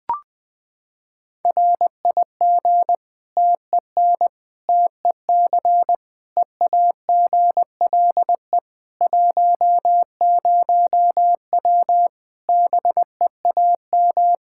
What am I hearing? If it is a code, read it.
Morse code, heard as a keyed tone, about 20 wpm, 700 Hz. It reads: RIG TEN TEC EAGLE 10W BEAM
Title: Practice QSO - Exchange 2 - Info - 20wpm